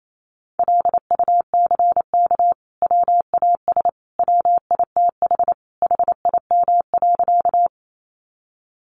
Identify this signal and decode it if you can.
Morse code, heard as a keyed tone, about 28 wpm, 700 Hz. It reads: LUCK WAS WITH HIM.
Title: Sentences from Top 1500 Words 28wpm